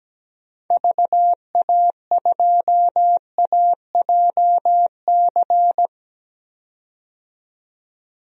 Morse code, heard as a keyed tone, about 17 wpm, 700 Hz. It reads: VA2AJC
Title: DXCC-100 Call Signs - 6 Characters - 17wpm